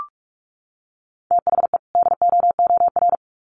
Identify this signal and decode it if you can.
Morse code, heard as a keyed tone, about 44 wpm, 700 Hz. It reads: THE DOOR
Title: Sets of 2 Words 45wpm